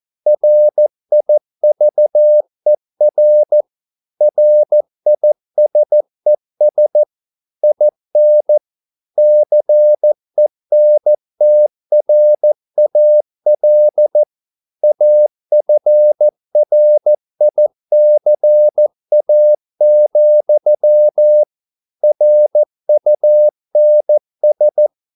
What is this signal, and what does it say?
Morse code, heard as a keyed tone, about 14 wpm, 600 Hz. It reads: RIVER RISES IN CENTRAL AFRICA, RUNS